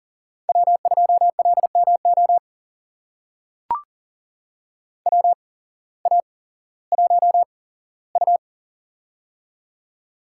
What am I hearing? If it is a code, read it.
Morse code, heard as a keyed tone, about 40 wpm, spaced as for 19 wpm, 700 Hz. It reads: W2LMO WA1U
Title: USA Call Signs - 8x Character Spacing - 40wpm